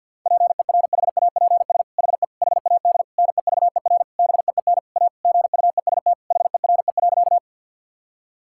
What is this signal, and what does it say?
Morse code, heard as a keyed tone, about 50 wpm, 700 Hz. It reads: PERHAPS HE HAD NEVER BEEN A GUEST HERE.
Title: Sentences from Top 1600 Words 50wpm